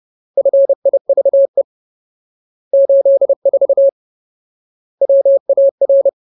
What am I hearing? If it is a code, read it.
Morse code, heard as a keyed tone, about 30 wpm, 550 Hz. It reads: FIVE 84 WAR
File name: Ultimate Word List / 30WPM Ultimate